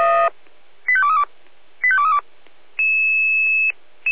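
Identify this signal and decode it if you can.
Five-tone selective call (ZVEI), heard as quick runs of five tones, tones of about 70 ms. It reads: 85232 85232
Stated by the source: ZVEI five-tone selective call 85810 received by SDR, from the Feuerwehr Orlamünde (fire brigade) audio server.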